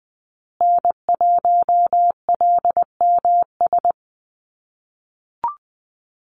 Morse code, heard as a keyed tone, about 20 wpm, 700 Hz. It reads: N1LMS
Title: USA Call Signs Repeated 2x 20wpm